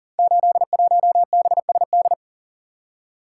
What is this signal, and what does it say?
Morse code, heard as a keyed tone, about 39 wpm, 700 Hz. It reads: 81BSD
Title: Code Groups - 5 Letter:Numbers - 40wpm